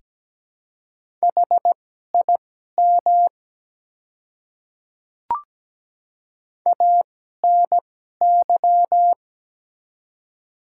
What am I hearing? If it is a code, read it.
Morse code, heard as a keyed tone, about 17 wpm, spaced as for 12 wpm, 700 Hz. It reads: HIM ANY
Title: 2x Character Spacing - Top 200 Words - 17wpm